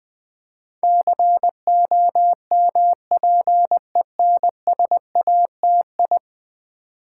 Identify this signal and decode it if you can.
Morse code, heard as a keyed tone, about 20 wpm, 700 Hz. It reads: COMPENSATI
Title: Build A Word - 5 Letters At A Time - 12-Letter Words - 20wpm